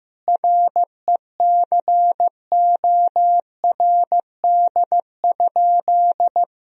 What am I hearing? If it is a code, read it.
Morse code, heard as a keyed tone, about 15 wpm, 700 Hz. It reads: RECORD?
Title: Sentences from Top 1000 Words - Review Part 4 - 15wpm